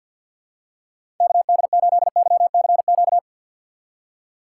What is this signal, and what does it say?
Morse code, heard as a keyed tone, about 46 wpm, 700 Hz. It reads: KD8YCQ
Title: USA Call Signs - 2x Character Spacing - 50wpm